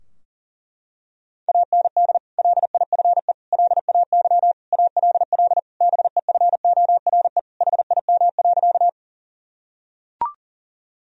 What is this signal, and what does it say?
Morse code, heard as a keyed tone, about 40 wpm, 700 Hz. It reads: AND LIFE LAY ALL BEFORE HIM.
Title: Sentences from Top 1000 Words - Review Part 2 - 40wpm